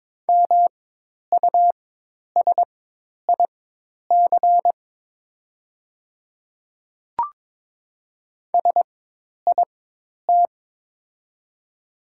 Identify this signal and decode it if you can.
Morse code, heard as a keyed tone, about 21 wpm, spaced as for 10 wpm, 700 Hz. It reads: MUSIC SIT
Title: 4x Character Spacing - Top 500 Words - 22wpm